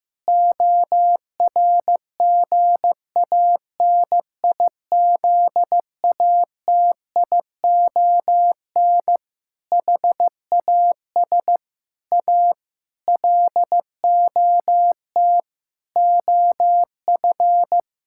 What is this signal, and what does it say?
Morse code, heard as a keyed tone, about 16 wpm, 700 Hz. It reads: ORGANIZATION HAS A LOT OF
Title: Sentences from Top 1000 Words - Review Part 6 - 15wpm